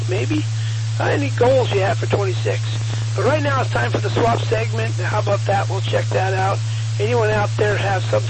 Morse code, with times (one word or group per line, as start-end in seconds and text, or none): none